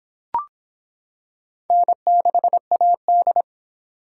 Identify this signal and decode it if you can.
Morse code, heard as a keyed tone, about 24 wpm, 700 Hz. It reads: N6AD